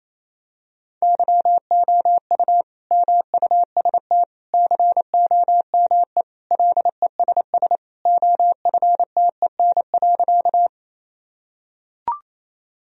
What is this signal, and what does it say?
Morse code, heard as a keyed tone, about 27 wpm, 700 Hz. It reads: YOU MUST COME LESS OFTEN.
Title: Sentences from Top 1000 Words - Review Part 2 - 28wpm